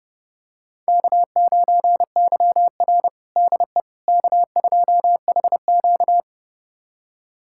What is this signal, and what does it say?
Morse code, heard as a keyed tone, about 31 wpm, 700 Hz. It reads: K9YR DE K2HQ
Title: USA Callsign DE Callsign - 4 Characters - 30wpm